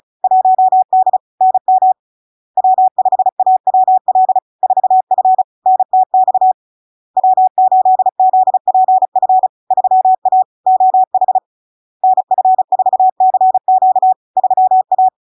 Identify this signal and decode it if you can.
Morse code, heard as a keyed tone, about 36 wpm, 750 Hz. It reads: 1D NM W5AWL 4F NTX W8ZPF 3A OH NF4CQ 3A
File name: LARK-FD-2022-Log-in-CW-35wpm-750-Hz